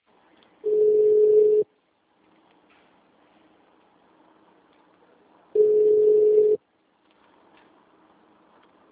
Ringback tone (Europe, 425 Hz).